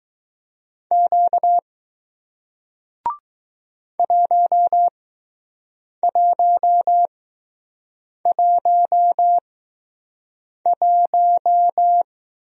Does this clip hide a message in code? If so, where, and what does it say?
Yes: Morse code from 0.9 to 12.0 s, heard as a keyed tone, about 20 wpm, spaced as for 6 wpm, 700 Hz. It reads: Q 1111